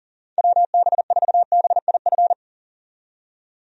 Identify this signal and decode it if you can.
Morse code, heard as a keyed tone, about 38 wpm, 700 Hz. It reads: WB4BIF
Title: USA Call Signs - 6 Characters - 40wpm